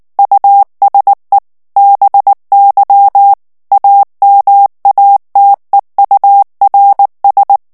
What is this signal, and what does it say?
Morse code, heard as a keyed tone, about 19 wpm, 800 Hz. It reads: USE BY AMATEURS
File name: ANGLAIS texte clair 3 vitesse 18 mots minute